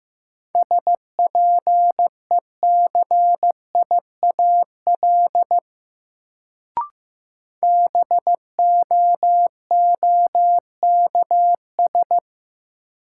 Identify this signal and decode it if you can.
Morse code, heard as a keyed tone, about 15 wpm, 700 Hz. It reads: SPECIAL BOOKS
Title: Sentences from Top 1200 Words - New Word Drill Alt - 15wpm